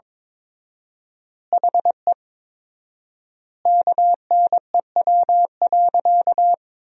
Morse code, heard as a keyed tone, about 22 wpm, 700 Hz. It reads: HE KNEW.
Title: 4x Word-Spacing - Sentences from Top 400 Words - 22wpm